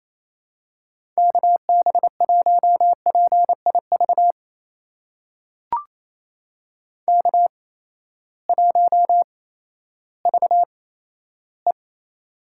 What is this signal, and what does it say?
Morse code, heard as a keyed tone, about 28 wpm, spaced as for 14 wpm, 700 Hz. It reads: KB1PIV K1VE
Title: USA Call Signs - 8x Character Spacing - 28wpm